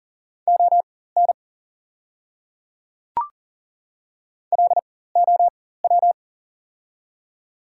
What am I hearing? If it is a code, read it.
Morse code, heard as a keyed tone, about 41 wpm, spaced as for 19 wpm, 690 Hz. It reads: ON LOW
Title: 4x Character Spacing - Top 300 Words - 40wpm